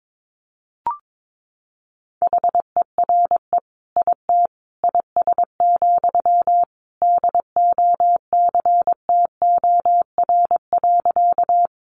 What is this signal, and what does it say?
Morse code, heard as a keyed tone, about 22 wpm, 700 Hz. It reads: HERE IT IS, DOCTOR.
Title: Sentences from Top 1500 Words 22wpm